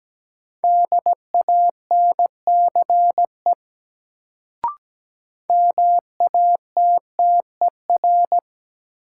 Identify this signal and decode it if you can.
Morse code, heard as a keyed tone, about 17 wpm, 700 Hz. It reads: DANCE MATTER
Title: Sentences from Top 700 Words - New Word Drill - 17wpm